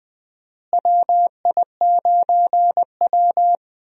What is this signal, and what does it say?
Morse code, heard as a keyed tone, about 20 wpm, 700 Hz. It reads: WI9W
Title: USA Call Signs - 4 Characters - Repeated 3x - 20wpm